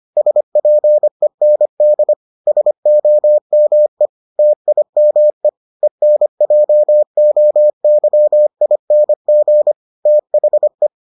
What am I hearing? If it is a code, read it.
Morse code, heard as a keyed tone, about 25 wpm, 600 Hz. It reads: SPEND SOME TIME ENJOYING THE